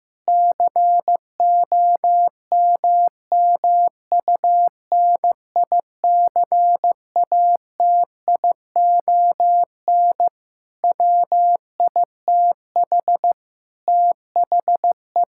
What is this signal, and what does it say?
Morse code, heard as a keyed tone, about 15 wpm, 700 Hz. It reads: COMMUNICATION WITH THE
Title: Sentences from Top 1000 Words - Review Part 6 - 15wpm